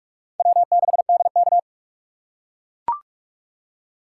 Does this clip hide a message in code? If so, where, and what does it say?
Yes: Morse code from 0.4 to 1.6 s, heard as a keyed tone, about 42 wpm, 700 Hz. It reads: W6DK